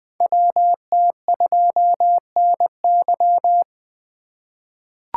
Morse code, heard as a keyed tone, about 20 wpm, 700 Hz. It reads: WT2NY
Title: USA Call Signs - 5 Characters - 2x Character Spacing - 20wpm